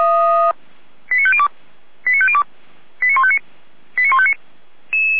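Five-tone selective call (ZVEI), heard as quick runs of five tones, tones of about 70 ms. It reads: 89502 89502 89159 89159
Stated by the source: ZVEI five-tone selective call 89099 received by SDR, served from the Feuerwehr Orlamünde audio server.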